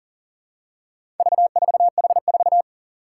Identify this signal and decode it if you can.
Morse code, heard as a keyed tone, about 40 wpm, 700 Hz. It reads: V4H4
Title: Four Challenging Letter-Number Combinations 40wpm